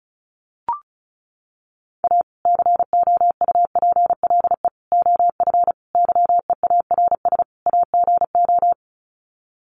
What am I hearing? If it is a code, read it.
Morse code, heard as a keyed tone, about 34 wpm, 700 Hz. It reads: A COUPLE OF YEARS AGO
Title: Sets of 5 Words 35wpm